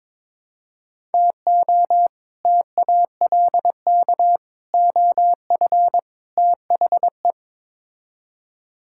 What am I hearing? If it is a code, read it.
Morse code, heard as a keyed tone, about 22 wpm, 700 Hz. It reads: TO TALK OF THE
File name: practice / 22wpm / Build A Phrase - 5-Word Phrases - 22wpm